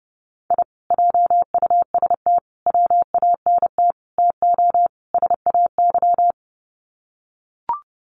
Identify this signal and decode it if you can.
Morse code, heard as a keyed tone, about 30 wpm, 700 Hz. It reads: I JUST WANT TO SAY
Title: Sets of 5 Words Encore 30wpm